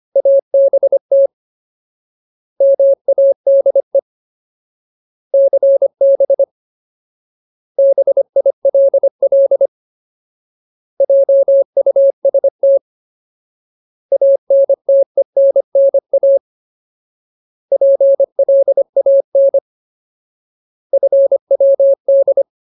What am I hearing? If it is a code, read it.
Morse code, heard as a keyed tone, about 25 wpm, 550 Hz. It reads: ABT MADE CB BILL JUST ANTENNA PLAN FWD
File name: Ultimate Word List / 25WPM Ultimate